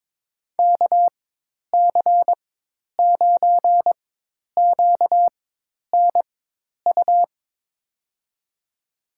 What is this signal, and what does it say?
Morse code, heard as a keyed tone, about 22 wpm, spaced as for 10 wpm, 700 Hz. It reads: KC9QNU
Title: USA Call Signs - 6 Characters - 4x Character Spacing - 22wpm